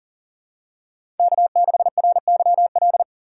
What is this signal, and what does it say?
Morse code, heard as a keyed tone, about 40 wpm, 700 Hz. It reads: K6RYL
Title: USA Call Signs - 2x Character Spacing - 40wpm